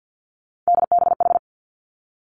Morse code, heard as a keyed tone, about 50 wpm, 700 Hz. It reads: DBH
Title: Three Challenging Letter-Number Combinations 50wpm